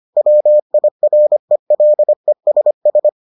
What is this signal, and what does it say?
Morse code, heard as a keyed tone, about 25 wpm, 600 Hz. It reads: WIRELESS